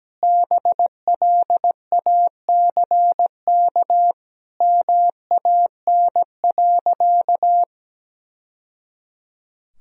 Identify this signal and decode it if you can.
Morse code, heard as a keyed tone, about 17 wpm, 700 Hz. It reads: BLACK MAN.
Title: Sentences from Top 1000 Words - Review Part 2 - 17wpm